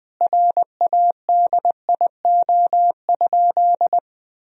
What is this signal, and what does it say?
Morse code, heard as a keyed tone, about 20 wpm, 700 Hz. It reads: RADIO?
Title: Sentences from Top 1000 Words - Review Part 3 - 20wpm